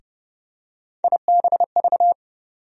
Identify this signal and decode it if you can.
Morse code, heard as a keyed tone, about 31 wpm, 700 Hz. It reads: IBV
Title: USA Call Sign Postfixes - 3 Letters 30wpm